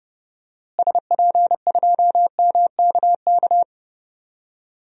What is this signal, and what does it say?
Morse code, heard as a keyed tone, about 31 wpm, 700 Hz. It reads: SP2MKK
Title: DXCC-100 Call Signs - 6 Characters - Repeated 3x - 30wpm